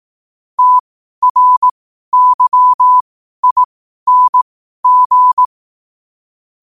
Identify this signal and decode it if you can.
Morse code, heard as a keyed tone, about 18 wpm, spaced as for 12 wpm, 1000 Hz. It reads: TRYING
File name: qotdfiles / 12wpm